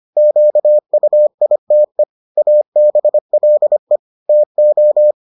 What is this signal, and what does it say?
Morse code, heard as a keyed tone, about 25 wpm, 600 Hz. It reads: QUITE ABLE TO